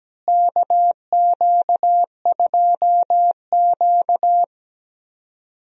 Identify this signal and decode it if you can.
Morse code, heard as a keyed tone, about 17 wpm, 700 Hz. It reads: KQ2Q
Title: USA Call Signs - 4 Characters - Repeated 2x - 17wpm